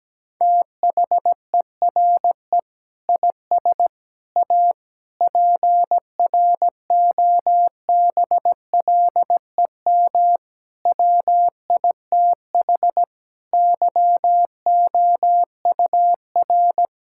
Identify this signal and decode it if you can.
Morse code, heard as a keyed tone, about 17 wpm, 700 Hz. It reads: THERE IS A PROBLEM WITH YOUR
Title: Sentences from Top 1000 Words - Review Part 5 - 17wpm